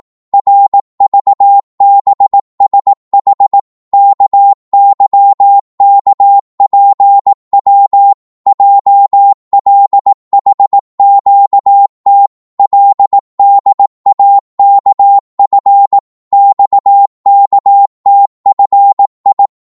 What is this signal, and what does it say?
Morse code, heard as a keyed tone, about 18 wpm, 800 Hz. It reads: RVBSH KYKPW JLHQT LDAKF XKTFI
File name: Morse / ltr-18wpm